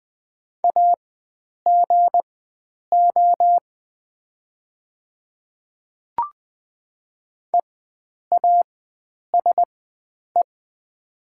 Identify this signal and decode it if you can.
Morse code, heard as a keyed tone, about 20 wpm, spaced as for 9 wpm, 700 Hz. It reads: AGO EASE
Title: 4x Character Spacing - Top 500 Words - 20wpm